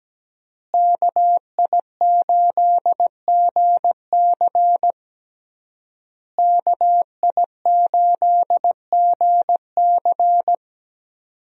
Morse code, heard as a keyed tone, about 17 wpm, 700 Hz. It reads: KI8GC KI8GC
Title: USA Call Signs - 5 Characters - Repeated 3x - 17wpm